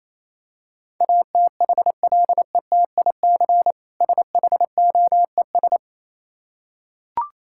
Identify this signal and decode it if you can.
Morse code, heard as a keyed tone, about 28 wpm, 700 Hz. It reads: ATHLETIC SHOES